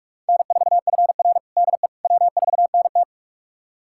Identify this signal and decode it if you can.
Morse code, heard as a keyed tone, about 45 wpm, 700 Hz. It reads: N4FR DE W4NT
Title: USA Callsign DE Callsign - 4 Characters - 45wpm